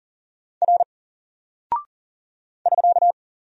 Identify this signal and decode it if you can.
Morse code, heard as a keyed tone, about 40 wpm, 700 Hz. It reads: R <SK>
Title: Single Character (Letter:Number:Punctuation:Prosign) 40wpm